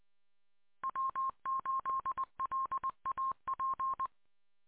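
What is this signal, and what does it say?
Morse code, heard as a keyed tone, about 20 wpm, 1060 Hz. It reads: W8LAP
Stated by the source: 146.620 MHz.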